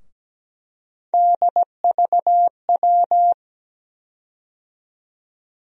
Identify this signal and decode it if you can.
Morse code, heard as a keyed tone, about 17 wpm, 700 Hz. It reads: DVW
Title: USA Call Sign Postfixes - 3 Letters 17wpm